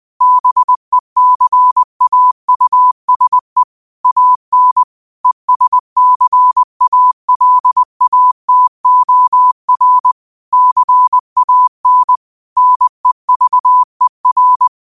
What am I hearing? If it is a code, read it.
Morse code, heard as a keyed tone, about 20 wpm, 1000 Hz. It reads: BECAUSE AN ESCALATOR CAN NEVER
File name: qotdfiles / 20wpm